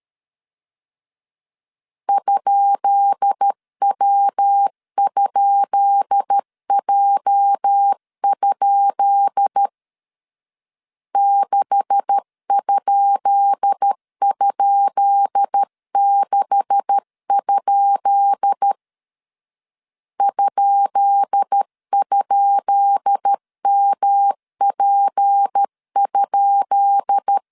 Morse code, heard as a keyed tone, about 13 wpm, 800 Hz. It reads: ?W?J? 6??6? ??MP?
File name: LEZIONI / 10 SIGNS / 10B INT 12wpm 2414